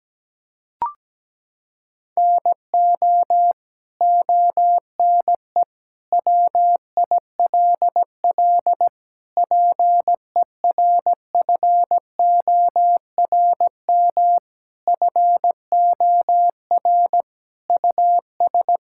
Morse code, heard as a keyed tone, about 17 wpm, 700 Hz. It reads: NO ONE WILL PERFORM FOR US
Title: Sentences from Top 1000 Words - Review Part 5 - 17wpm